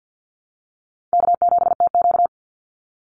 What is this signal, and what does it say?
Morse code, heard as a keyed tone, about 53 wpm, 700 Hz. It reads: K7TQ
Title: NA SOTA Chaser Callsigns 50wpm